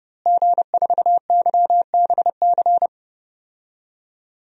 Morse code, heard as a keyed tone, about 30 wpm, 700 Hz. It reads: G4YBC
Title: DXCC-100 Call Signs - 5 Characters - Repeated 2x - 30wpm